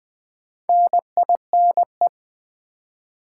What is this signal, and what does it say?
Morse code, heard as a keyed tone, about 20 wpm, 700 Hz. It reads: NINE
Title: Build A Word - 2 Letters At A Time - 8-Letter Words - 20wpm